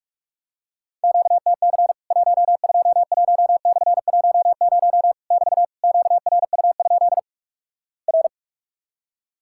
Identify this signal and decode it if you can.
Morse code, heard as a keyed tone, about 46 wpm, 700 Hz. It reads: QTC 121/10 <BT> QRU? R
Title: Worked All Europe DX Contest - QTC Exchange - 45wpm